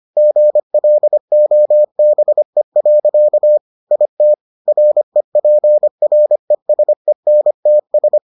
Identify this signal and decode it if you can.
Morse code, heard as a keyed tone, about 25 wpm, 600 Hz. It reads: GLOBE. IT REPRESENTS